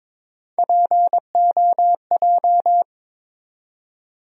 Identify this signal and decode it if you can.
Morse code, heard as a keyed tone, about 22 wpm, 700 Hz. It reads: POJ
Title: USA Call Sign Postfixes - 3 Letters 22wpm